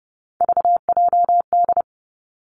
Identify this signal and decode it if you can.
Morse code, heard as a keyed tone, about 30 wpm, 700 Hz. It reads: VJD
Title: Three Challenging Letter-Number Combinations 30wpm